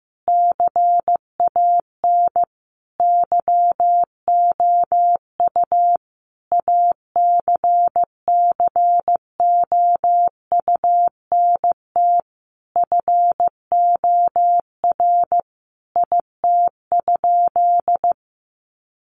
Morse code, heard as a keyed tone, about 15 wpm, 700 Hz. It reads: CAN YOU ACCOUNT FOR IT?
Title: Sentences from Top 900 Words 15wpm